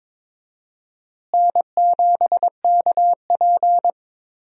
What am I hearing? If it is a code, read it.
Morse code, heard as a keyed tone, about 21 wpm, 700 Hz. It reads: N7KP
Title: USA Call Signs - 4 Characters - Repeated 2x - 22wpm